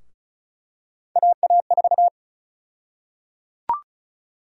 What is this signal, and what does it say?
Morse code, heard as a keyed tone, about 36 wpm, 700 Hz. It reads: AA4
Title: USA Call Sign Prefixes - Repeated 2x - 35wpm